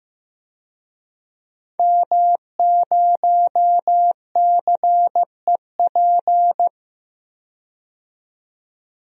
Morse code, heard as a keyed tone, about 15 wpm, 700 Hz. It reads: M0CEP
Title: DXCC-100 Call Signs - 5 Characters - 15wpm